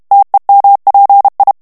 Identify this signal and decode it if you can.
Morse code, heard as a keyed tone, about 32 wpm, 800 Hz. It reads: TEMPI
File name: ITALIEN texte clair 1 vitesse 30 mots minute